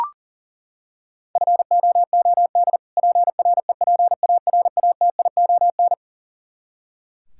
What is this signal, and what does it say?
Morse code, heard as a keyed tone, about 40 wpm, 700 Hz. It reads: FOOD PREPARATION
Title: Practice QSO - Exchange 3 - More Info - 40wpm